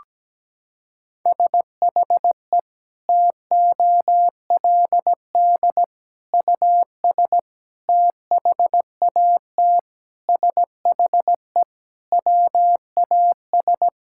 Morse code, heard as a keyed tone, about 17 wpm, 700 Hz. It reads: SHE TOLD US THAT SHE WAS
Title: Sentences from Top 1000 Words - Review Part 6 - 17wpm